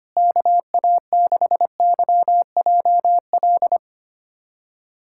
Morse code, heard as a keyed tone, about 25 wpm, 700 Hz. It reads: KA6YJL
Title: USA Callsign DE Callsign - 6 Characters - 25wpm